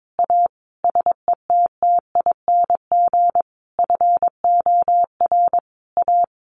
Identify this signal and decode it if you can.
Morse code, heard as a keyed tone, about 22 wpm, 700 Hz. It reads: A SETTING FOR A
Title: Sentences from Top 1300 Words 22wpm